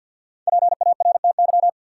Morse code, heard as a keyed tone, about 52 wpm, 700 Hz. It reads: PARTY
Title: Sentences from Top 1400 Words - New Word Drill - 50wpm